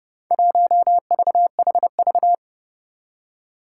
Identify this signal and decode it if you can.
Morse code, heard as a keyed tone, about 30 wpm, 700 Hz. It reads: 1VHV
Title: Four Challenging Letter-Number Combinations 30wpm